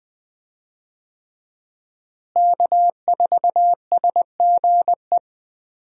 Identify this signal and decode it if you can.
Morse code, heard as a keyed tone, about 20 wpm, 700 Hz. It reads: K4SGE
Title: USA Call Signs - 5 Characters - Repeated 3x - 20wpm